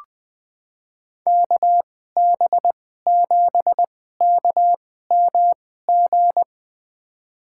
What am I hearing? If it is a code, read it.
Morse code, heard as a keyed tone, about 21 wpm, spaced as for 15 wpm, 700 Hz. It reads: KB7KMG
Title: USA Call Signs - 6 Characters - 2x Character Spacing - 20wpm